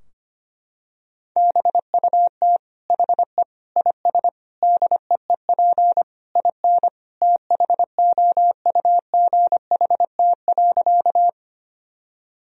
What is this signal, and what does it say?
Morse code, heard as a keyed tone, about 25 wpm, 700 Hz. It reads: BUT HE IS DEEP IN THOUGHT.
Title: Sentences from Top 1000 Words - Review Part 3 - 25wpm